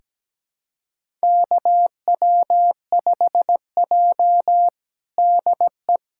Morse code, heard as a keyed tone, about 17 wpm, 700 Hz. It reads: KW5J DE
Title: USA Callsign DE Callsign - 4 Characters - 17wpm